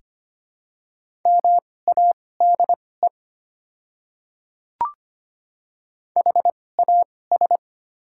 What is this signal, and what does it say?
Morse code, heard as a keyed tone, about 25 wpm, spaced as for 18 wpm, 700 Hz. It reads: MADE HAS
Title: 2x Character Spacing - Top 300 Words - 25wpm